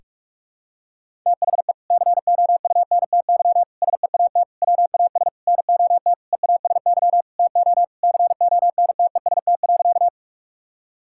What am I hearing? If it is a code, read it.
Morse code, heard as a keyed tone, about 46 wpm, 700 Hz. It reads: THE COUNTY SEAT WAS NOT EASY TO CONTEST.
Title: Sentences from Top 1600 Words 45wpm